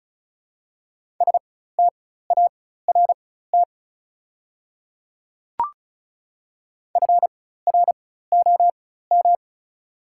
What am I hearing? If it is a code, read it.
Morse code, heard as a keyed tone, about 34 wpm, spaced as for 16 wpm, 700 Hz. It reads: START FROM
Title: 4x Character Spacing - Top 500 Words - 35wpm